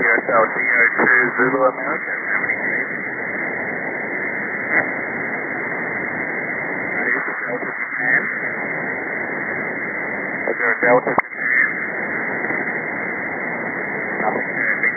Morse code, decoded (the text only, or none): none